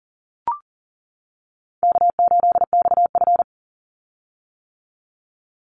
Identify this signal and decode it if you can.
Morse code, heard as a keyed tone, about 40 wpm, 700 Hz. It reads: K8XF
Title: Call Signs - 40wpm